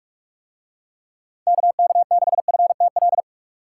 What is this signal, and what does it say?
Morse code, heard as a keyed tone, about 42 wpm, 700 Hz. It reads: KK6FTL